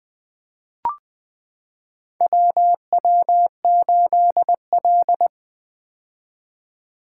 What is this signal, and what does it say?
Morse code, heard as a keyed tone, about 21 wpm, 700 Hz. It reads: WW8L
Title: 2023 POTA Activator Call Signs - 20wpm